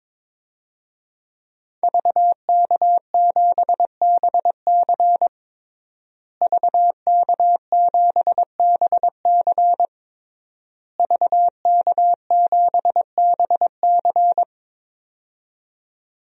Morse code, heard as a keyed tone, about 22 wpm, 700 Hz. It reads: VK7BC VK7BC VK7BC
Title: DXCC-100 Call Signs - 5 Characters - Repeated 3x - 22wpm